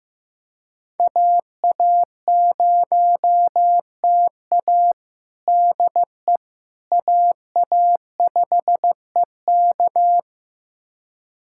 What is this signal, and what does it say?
Morse code, heard as a keyed tone, about 15 wpm, 700 Hz. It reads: AA0TA DE AA5EK
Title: USA Callsign DE Callsign - 5 Characters - 15wpm